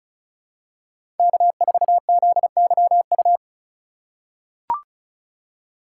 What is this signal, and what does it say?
Morse code, heard as a keyed tone, about 36 wpm, 700 Hz. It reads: K4ZYU
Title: USA Call Signs - 5 Characters - 3x Character Spacing - 35wpm